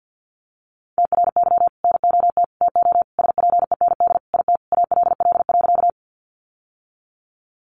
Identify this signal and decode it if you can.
Morse code, heard as a keyed tone, about 50 wpm, 700 Hz. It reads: TRY NOT TO SPEND IT ALL.
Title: Sentences from Top 800 Words 50wpm